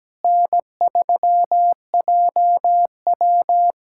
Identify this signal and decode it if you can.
Morse code, heard as a keyed tone, about 17 wpm, 700 Hz. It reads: N3JW
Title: USA Callsign DE Callsign - 4 Characters - 17wpm